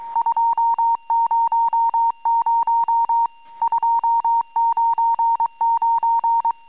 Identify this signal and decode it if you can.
Morse code, heard as a keyed tone, about 23 wpm, 910 Hz. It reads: 200 299
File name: amsat / cw